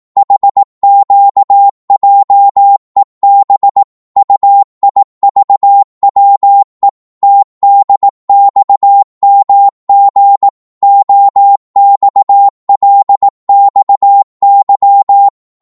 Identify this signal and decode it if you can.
Morse code, heard as a keyed tone, about 18 wpm, 800 Hz. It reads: HQJEB UIVWE TDXMG OXLXY
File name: Morse / ltr-18wpm